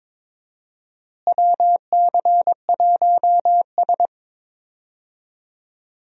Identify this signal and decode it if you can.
Morse code, heard as a keyed tone, about 22 wpm, 700 Hz. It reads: WC1S
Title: USA Call Signs - 4 Characters - Repeated 3x - 22wpm